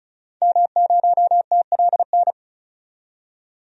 Morse code, heard as a keyed tone, about 36 wpm, 700 Hz. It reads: M0TLN
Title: DXCC-100 Call Signs - 5 Characters - 35wpm